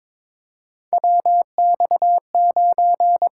Morse code, heard as a keyed tone, about 22 wpm, 700 Hz. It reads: WX9
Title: USA Call Signs - 5 Characters - 8x Character Spacing - 22wpm